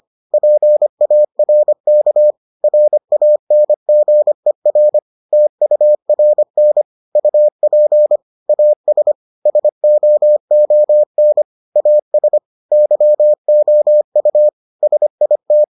Morse code, heard as a keyed tone, about 25 wpm, 600 Hz. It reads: PARK RANGER TURN UP AS SOON AS YOU SIT